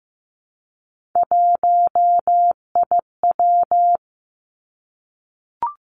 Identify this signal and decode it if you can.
Morse code, heard as a keyed tone, about 15 wpm, 700 Hz. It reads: 1IW
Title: Three Challenging Letter-Number Combinations 15wpm